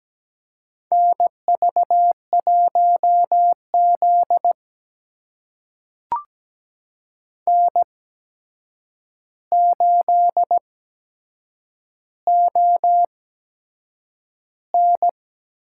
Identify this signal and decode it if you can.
Morse code, heard as a keyed tone, about 17 wpm, spaced as for 7 wpm, 700 Hz. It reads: NV1Z N8ON
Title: USA Call Signs - 4 Characters - 8x Character Spacing - 17wpm